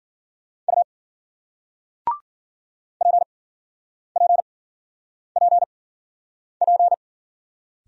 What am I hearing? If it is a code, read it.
Morse code, heard as a keyed tone, about 53 wpm, spaced as for 9 wpm, 700 Hz. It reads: U PPPP